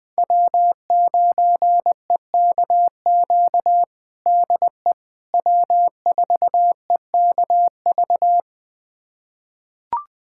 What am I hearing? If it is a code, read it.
Morse code, heard as a keyed tone, about 20 wpm, 700 Hz. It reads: W9EKQ DE W4EKV
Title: USA Callsign DE Callsign - 5 Characters - 20wpm